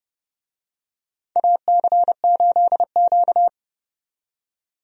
Morse code, heard as a keyed tone, about 30 wpm, 700 Hz. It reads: AC8Q
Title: USA Call Signs - 4 Characters - 30wpm